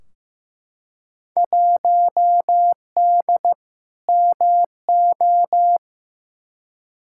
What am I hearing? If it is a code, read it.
Morse code, heard as a keyed tone, about 16 wpm, 700 Hz. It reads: 1D MO
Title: Field Day Exchanges - 15wpm